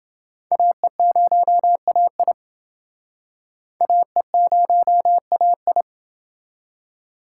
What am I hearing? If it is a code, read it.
Morse code, heard as a keyed tone, about 28 wpm, 700 Hz. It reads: AE0AI AE0AI